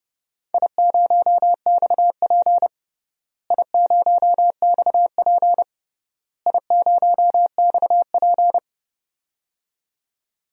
Morse code, heard as a keyed tone, about 32 wpm, 700 Hz. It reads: I0XP I0XP I0XP
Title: DXCC-100 Call Signs - 4 Characters - Repeated 3x - 30wpm